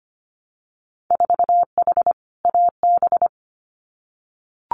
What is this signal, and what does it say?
Morse code, heard as a keyed tone, about 25 wpm, 700 Hz. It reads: 4H AB
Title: Winter Field Day Exchanges Encore 25wpm